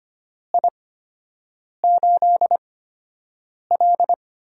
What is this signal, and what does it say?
Morse code, heard as a keyed tone, about 25 wpm, spaced as for 7 wpm, 700 Hz. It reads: I8L
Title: USA Call Signs - 4 Characters - 8x Character Spacing - 25wpm